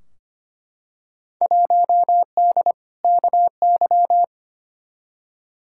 Morse code, heard as a keyed tone, about 25 wpm, 700 Hz. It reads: 1D KY
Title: Field Day Exchanges - 25wpm